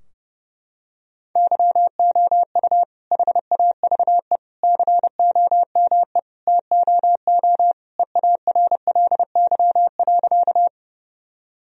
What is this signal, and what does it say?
Morse code, heard as a keyed tone, about 30 wpm, 700 Hz. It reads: YOU HAVE COME TOO EARLY.
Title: Sentences from Top 1000 Words - Review Part 2 - 30wpm